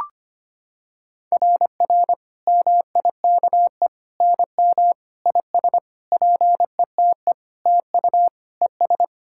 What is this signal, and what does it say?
Morse code, heard as a keyed tone, about 25 wpm, 700 Hz. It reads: RR MIKE NM IS PETE TU ES
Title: SOTA QSO Practice 25wpm